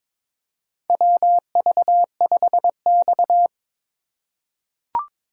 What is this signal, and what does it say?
Morse code, heard as a keyed tone, about 22 wpm, 700 Hz. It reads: WV5X
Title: USA Call Signs - 4x Character Spacing - 22wpm